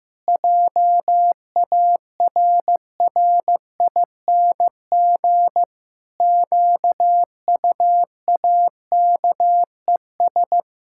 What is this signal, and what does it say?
Morse code, heard as a keyed tone, about 16 wpm, 700 Hz. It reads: JARRING QUAKES
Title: Pangram Challenge v2 - 15wpm